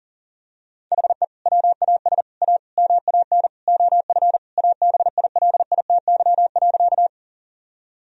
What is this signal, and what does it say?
Morse code, heard as a keyed tone, about 37 wpm, 700 Hz. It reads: HE WAS A MAN OF ABILITY.
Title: Sentences from Top 1000 Words - Review Part 3 - 40wpm